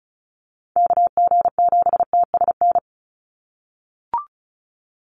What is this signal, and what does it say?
Morse code, heard as a keyed tone, about 34 wpm, 700 Hz. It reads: KG7TSN
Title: USA Call Signs 35wpm